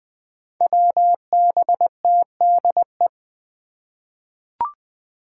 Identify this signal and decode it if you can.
Morse code, heard as a keyed tone, about 20 wpm, 700 Hz. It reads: WBTDE
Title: Five Challenging Letter-Number Combinations 20wpm